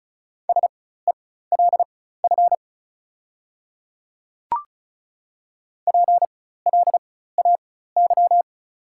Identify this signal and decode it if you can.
Morse code, heard as a keyed tone, about 34 wpm, spaced as for 16 wpm, 700 Hz. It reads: SELF PLAY
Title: 4x Character Spacing - Top 200 Words - 35wpm